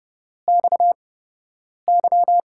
Morse code, heard as a keyed tone, about 30 wpm, spaced as for 8 wpm, 700 Hz. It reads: XY
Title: USA Call Signs - 5 Characters - 8x Character Spacing - 30wpm